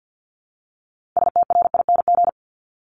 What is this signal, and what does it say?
Morse code, heard as a keyed tone, about 50 wpm, 700 Hz. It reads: STRING
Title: Multi-letter Phonemes 50wpm